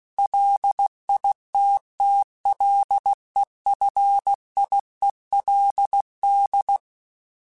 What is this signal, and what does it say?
Morse code, heard as a keyed tone, about 16 wpm, 800 Hz. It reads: LITTLEFIELD
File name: fromSMs / FromSMs mp3s / Cities@15